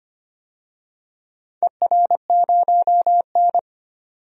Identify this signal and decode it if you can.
Morse code, heard as a keyed tone, about 25 wpm, 700 Hz. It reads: ER0N
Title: DXCC-100 Call Signs - 4 Characters - 25wpm